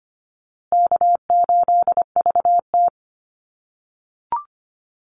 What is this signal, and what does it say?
Morse code, heard as a keyed tone, about 25 wpm, 700 Hz. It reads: K8VT
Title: USA Call Signs Repeated 2x 25wpm